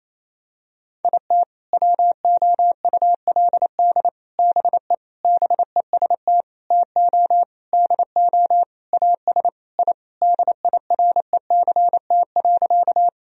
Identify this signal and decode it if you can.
Morse code, heard as a keyed tone, about 27 wpm, 700 Hz. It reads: IT WOULD BE BEST TO DO AS I DIRECT.
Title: Sentences from Top 1000 Words - Review Part 2 - 28wpm